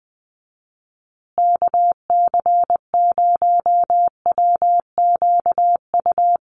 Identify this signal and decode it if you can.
Morse code, heard as a keyed tone, about 20 wpm, 700 Hz. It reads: KC0WQU
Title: USA Call Signs Repeated 2x 20wpm